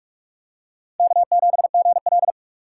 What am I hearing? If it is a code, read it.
Morse code, heard as a keyed tone, about 48 wpm, 700 Hz. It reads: K7GL